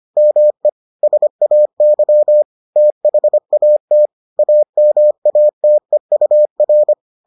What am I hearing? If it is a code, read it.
Morse code, heard as a keyed tone, about 25 wpm, 600 Hz. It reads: ME SAY THAT AMATEUR